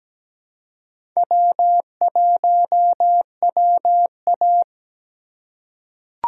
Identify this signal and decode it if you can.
Morse code, heard as a keyed tone, about 17 wpm, 700 Hz. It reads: W1WA
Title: USA Call Signs - 4 Characters - 2x Character Spacing - 17wpm